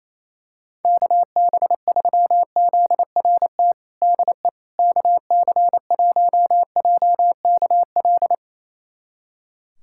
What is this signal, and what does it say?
Morse code, heard as a keyed tone, about 28 wpm, 700 Hz. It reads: KB3ZRT DE KC1JKL
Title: USA Callsign DE Callsign 28wpm